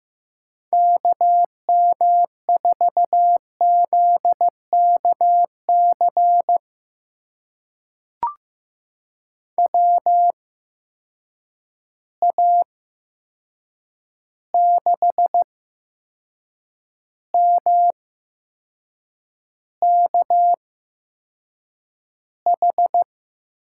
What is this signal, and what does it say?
Morse code, heard as a keyed tone, about 15 wpm, spaced as for 6 wpm, 700 Hz. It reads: KM4ZKC WA6MKH
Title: USA Call Signs - 6 Characters - 8x Character Spacing - 15wpm